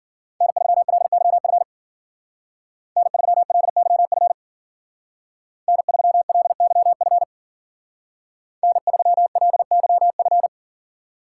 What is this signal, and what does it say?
Morse code, heard as a keyed tone, about 48 wpm, 700 Hz. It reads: N3LYF N3LYF N3LYF N3LYF
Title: Speed Racing - USA 1x3 Call Signs 40wpm